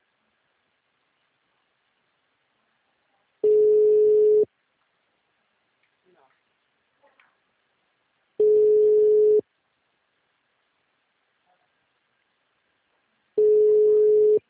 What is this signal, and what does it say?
Ringback tone (Europe, 425 Hz), with speech.